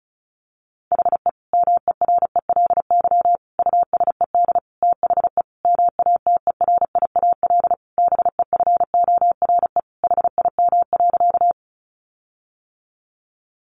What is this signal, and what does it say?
Morse code, heard as a keyed tone, about 34 wpm, 700 Hz. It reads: HE MERELY USED THE MATERIAL BEFORE HIM.
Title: Sentences from Top 1400 Words 35wpm